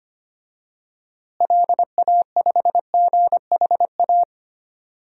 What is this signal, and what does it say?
Morse code, heard as a keyed tone, about 25 wpm, 700 Hz. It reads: LA5GHA
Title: DXCC-100 Call Signs - 6 Characters - Repeated 3x - 25wpm